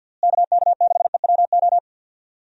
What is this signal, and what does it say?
Morse code, heard as a keyed tone, about 50 wpm, 700 Hz. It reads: KK6EWO
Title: USA Call Signs - 6 Characters - Repeated 2x - 50wpm